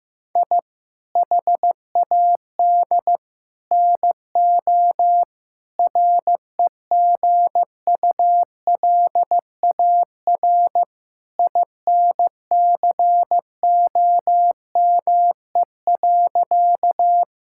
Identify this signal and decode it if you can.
Morse code, heard as a keyed tone, about 16 wpm, 700 Hz. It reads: I HAD NO REGULAR INCOME.
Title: Sentences from Top 1000 Words - Review Part 6 - 15wpm